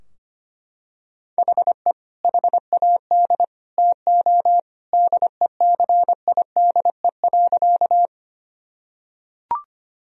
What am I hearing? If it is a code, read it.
Morse code, heard as a keyed tone, about 25 wpm, 700 Hz. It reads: HE HAD TO DECIDE.
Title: Sentences from Top 1000 Words - Review Part 3 - 25wpm